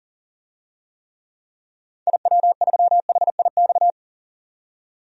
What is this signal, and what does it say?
Morse code, heard as a keyed tone, about 37 wpm, 700 Hz. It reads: IW3HIX
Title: DXCC-100 Call Signs - 6 Characters - 40wpm